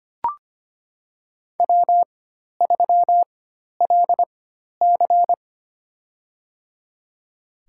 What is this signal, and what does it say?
Morse code, heard as a keyed tone, about 25 wpm, spaced as for 12 wpm, 700 Hz. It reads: W3LC
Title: USA Call Signs - 4 Characters - 4x Character Spacing - 25wpm